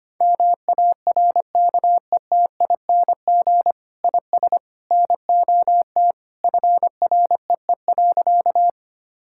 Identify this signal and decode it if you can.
Morse code, heard as a keyed tone, about 25 wpm, 700 Hz. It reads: MARKETING IS NOT FREE.
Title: Sentences from Top 1000 Words - Review Part 5 - 25wpm